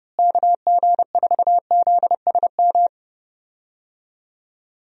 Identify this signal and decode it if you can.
Morse code, heard as a keyed tone, about 30 wpm, 700 Hz. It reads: KG4ZSM
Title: USA Call Signs - 6 Characters - Repeated 2x - 30wpm